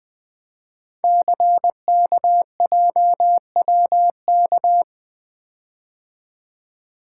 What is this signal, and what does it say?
Morse code, heard as a keyed tone, about 20 wpm, 700 Hz. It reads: CKJWK
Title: Code Groups - 5 Letters - 20wpm